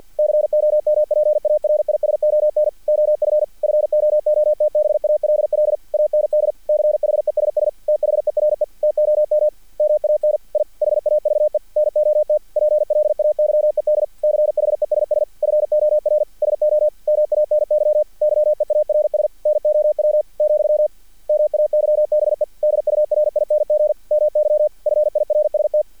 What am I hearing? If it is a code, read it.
Morse code, heard as a keyed tone, about 50 wpm, 600 Hz. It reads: COMPARISON OF FOOTBALL AND CHESS THERE TOM MAN I HAVE NOT PLAYED CHESS FOR SO MANY YEARS NOW , MAYBE DURING MY FIRST
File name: archived / CRSnet